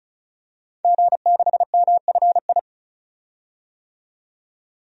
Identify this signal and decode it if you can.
Morse code, heard as a keyed tone, about 36 wpm, 700 Hz. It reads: G6MFI
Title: DXCC-100 Call Signs - 5 Characters - 35wpm